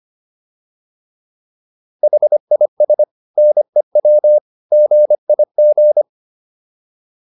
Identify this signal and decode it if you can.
Morse code, heard as a keyed tone, about 25 wpm, 600 Hz. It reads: HIS NEW GIG